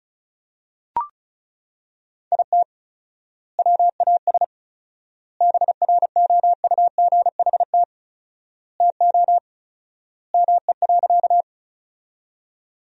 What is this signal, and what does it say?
Morse code, heard as a keyed tone, about 36 wpm, 700 Hz. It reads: IT WAS BROUGHT TO ME.
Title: 4x Word-Spacing - Sentences from Top 500 Words - 35wpm